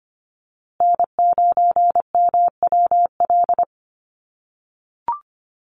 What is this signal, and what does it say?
Morse code, heard as a keyed tone, about 25 wpm, 700 Hz. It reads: N9MWL
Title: USA Call Signs Repeated 2x 25wpm